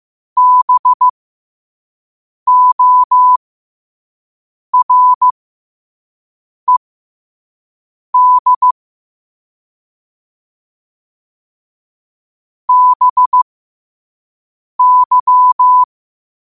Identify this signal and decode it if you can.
Morse code, heard as a keyed tone, about 15 wpm, spaced as for 5 wpm, 1000 Hz. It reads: BORED BY
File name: qotdfiles / 5wpm